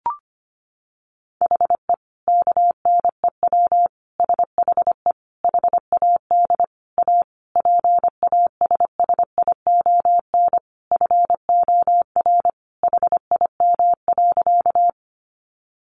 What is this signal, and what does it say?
Morse code, heard as a keyed tone, about 25 wpm, 700 Hz. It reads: HE KNEW SHE HAD A PASSION FOR HIM.
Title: Sentences from Top 1300 Words 25wpm